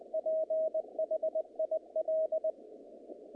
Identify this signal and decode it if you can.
Morse code, heard as a keyed tone, about 21 wpm, 630 Hz. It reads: PHIL
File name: contests / naqp-cw-k5tr-op-k5pi- -2200